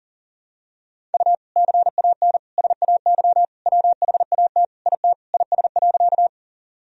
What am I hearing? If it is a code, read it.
Morse code, heard as a keyed tone, about 40 wpm, 700 Hz. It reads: U CAN SAY WHAT IT IS.
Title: Sentences from Top 200 Words and Abbreviations 40wpm